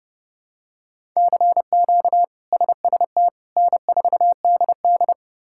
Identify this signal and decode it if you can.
Morse code, heard as a keyed tone, about 30 wpm, 700 Hz. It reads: CQ SST N4DD
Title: K1USN SST QSO Practice - 30wpm